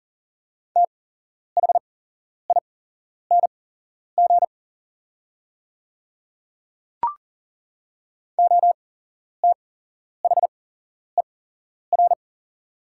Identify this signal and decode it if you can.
Morse code, heard as a keyed tone, about 40 wpm, spaced as for 11 wpm, 700 Hz. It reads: THING OTHER
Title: 8x Character Spacing - Top 100 Words - 40wpm